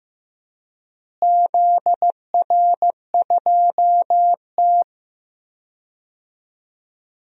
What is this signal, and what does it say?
Morse code, heard as a keyed tone, about 15 wpm, 700 Hz. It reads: ZR2T
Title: DXCC-100 Call Signs - 4 Characters - Repeated 3x - 15wpm